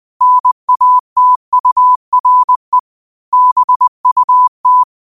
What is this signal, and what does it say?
Morse code, heard as a keyed tone, about 20 wpm, 1000 Hz. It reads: NATURE BUT
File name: qotdfiles / 20wpm